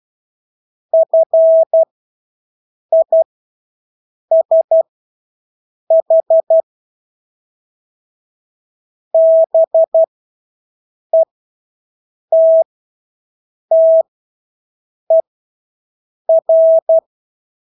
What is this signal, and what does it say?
Morse code, heard as a keyed tone, about 12 wpm, spaced as for 6 wpm, 650 Hz. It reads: FISH BETTER
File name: Engelske ord 12-6 wpm